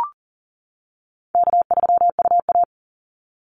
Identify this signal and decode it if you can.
Morse code, heard as a keyed tone, about 39 wpm, 700 Hz. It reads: K3UA
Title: Call Signs - 40wpm